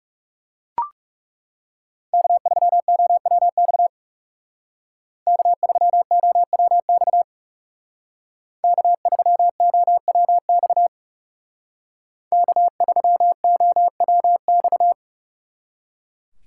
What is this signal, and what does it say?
Morse code, heard as a keyed tone, about 39 wpm, 700 Hz. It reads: K3OWX K3OWX K3OWX K3OWX